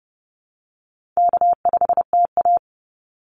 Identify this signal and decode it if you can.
Morse code, heard as a keyed tone, about 30 wpm, 700 Hz. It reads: K5TA
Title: Easy Call Signs Repeated 2x - 30wpm